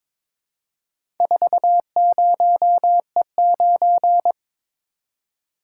Morse code, heard as a keyed tone, about 22 wpm, 700 Hz. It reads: 40E9
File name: practice / 22wpm / Four Letter-Number Combinations - 22wpm